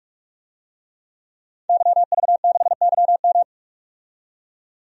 Morse code, heard as a keyed tone, about 41 wpm, 700 Hz. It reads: YV6YM
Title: DXCC-100 Call Signs - 5 Characters - 45wpm